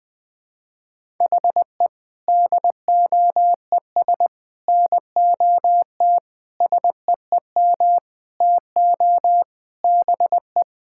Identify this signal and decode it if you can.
Morse code, heard as a keyed tone, about 20 wpm, 700 Hz. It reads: HE DOES NOT SEEM TO BE
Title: Sentences with 5-Word Phrases - Spoken Beginning - 20wpm